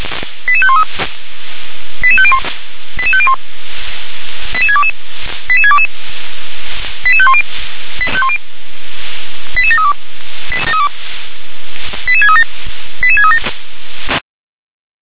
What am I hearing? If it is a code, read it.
Five-tone selective call (ZVEI), heard as quick runs of five tones, tones of about 70 ms. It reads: 88513 88501 88501 88622 88622 88510 88732 88637 88637
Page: SDR/ZVEI